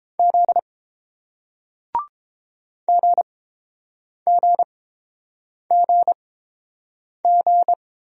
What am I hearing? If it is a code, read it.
Morse code, heard as a keyed tone, about 30 wpm, spaced as for 7 wpm, 700 Hz. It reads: Z GGGG